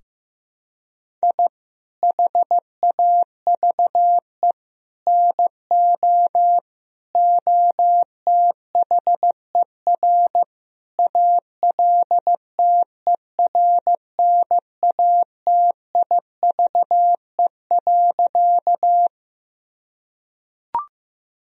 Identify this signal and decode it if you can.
Morse code, heard as a keyed tone, about 15 wpm, 700 Hz. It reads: I HAVE NO OTHER ALTERNATIVE.
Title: Sentences from Top 1000 Words - Review Part 4 - 15wpm